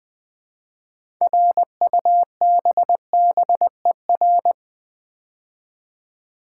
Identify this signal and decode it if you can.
Morse code, heard as a keyed tone, about 20 wpm, 700 Hz. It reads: RUBBER
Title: Build A Word - 3 Letters At A Time - 6-Letter Words - 20wpm